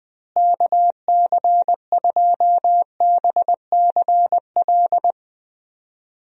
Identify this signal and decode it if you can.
Morse code, heard as a keyed tone, about 20 wpm, 700 Hz. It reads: KC2BCL
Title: USA Call Signs - 6 Characters - Repeated 2x - 20wpm